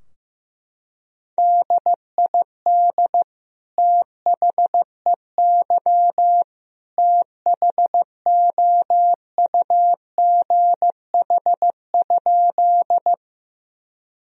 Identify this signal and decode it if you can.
Morse code, heard as a keyed tone, about 15 wpm, 700 Hz. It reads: DID THEY THOUGH?
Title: Sentences from Top 1000 Words - Review Part 2 - 15wpm